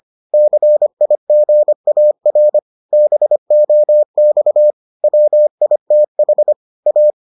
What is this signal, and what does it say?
Morse code, heard as a keyed tone, about 25 wpm, 600 Hz. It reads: CIGAR BOX WITH A